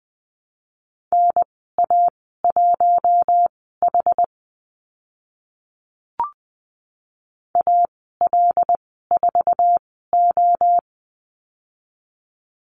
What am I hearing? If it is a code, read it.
Morse code, heard as a keyed tone, about 20 wpm, spaced as for 14 wpm, 700 Hz. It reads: NA1H AL4O
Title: 076 Lesson 13g - Letter L - Call Signs 20wpm